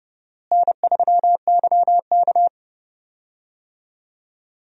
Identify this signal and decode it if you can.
Morse code, heard as a keyed tone, about 31 wpm, 700 Hz. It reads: N3YK
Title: USA Call Signs - 4 Characters - Repeated 2x - 30wpm